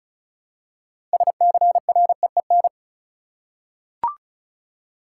Morse code, heard as a keyed tone, about 36 wpm, 700 Hz. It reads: SCREEN
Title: Six Letter Words - 35wpm